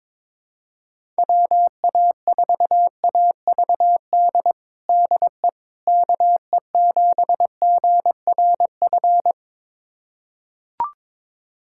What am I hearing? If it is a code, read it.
Morse code, heard as a keyed tone, about 22 wpm, 700 Hz. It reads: WA4AVD DE KE7GRF
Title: USA Callsign DE Callsign 22wpm